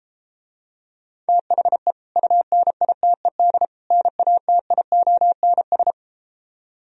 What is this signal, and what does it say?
Morse code, heard as a keyed tone, about 33 wpm, 700 Hz. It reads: THE UNITED NATIONS